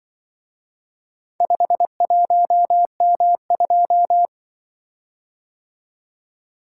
Morse code, heard as a keyed tone, about 23 wpm, 700 Hz. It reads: 51M2
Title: Speed-Racing - Four Letter-Number Combinations - 20wpm